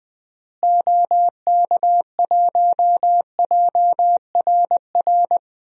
Morse code, heard as a keyed tone, about 21 wpm, 700 Hz. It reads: OK1JRR
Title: DXCC-100 Call Signs - 6 Characters - Repeated 3x - 20wpm